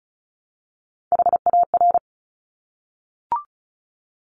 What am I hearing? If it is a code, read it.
Morse code, heard as a keyed tone, about 36 wpm, 700 Hz. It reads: HAR
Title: Top Three Letter Combinations in Words 35wpm